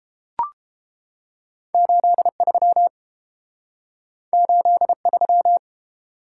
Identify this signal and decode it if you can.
Morse code, heard as a keyed tone, about 29 wpm, 700 Hz. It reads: 83 83